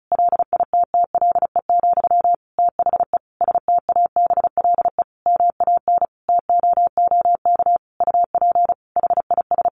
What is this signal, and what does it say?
Morse code, heard as a keyed tone, about 34 wpm, 700 Hz. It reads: LITTLE, THE STABLE MAN TOOK UP HIS
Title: Sentences from Top 1500 Words 35wpm